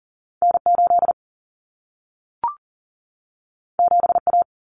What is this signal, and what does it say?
Morse code, heard as a keyed tone, about 40 wpm, 700 Hz. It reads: N8 7A
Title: Common Cut Numbers - 2-Digit 40wpm